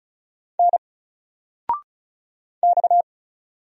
Morse code, heard as a keyed tone, about 36 wpm, 700 Hz. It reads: N X
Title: Single Character (Letter:Number:Punctuation:Prosign) 35wpm